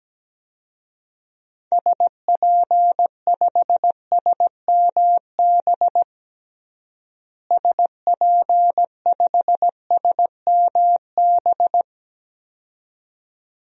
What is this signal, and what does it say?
Morse code, heard as a keyed tone, about 17 wpm, 700 Hz. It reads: SP5SMB SP5SMB
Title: DXCC-100 Call Signs - 6 Characters - Repeated 2x - 17wpm